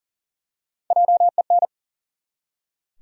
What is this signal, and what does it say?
Morse code, heard as a keyed tone, about 40 wpm, 700 Hz. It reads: JEN
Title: USA Common Names - Max 3 Letters - 40wpm